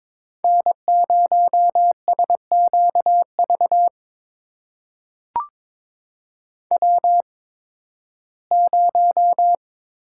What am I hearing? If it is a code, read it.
Morse code, heard as a keyed tone, about 23 wpm, spaced as for 14 wpm, 700 Hz. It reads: N0SQV W0
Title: USA Call Signs - 5 Characters - 8x Character Spacing - 22wpm